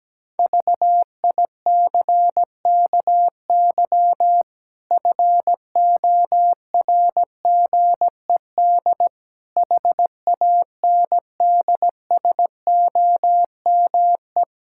Morse code, heard as a keyed tone, about 17 wpm, 700 Hz. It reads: VICKY FORGED HANDSOME
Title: Pangram Challenge v2 - 17wpm